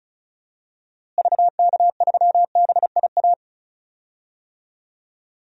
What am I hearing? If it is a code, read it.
Morse code, heard as a keyed tone, about 34 wpm, 700 Hz. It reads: VK3BIA
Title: DXCC-100 Call Signs - 6 Characters - Repeated 2x - 35wpm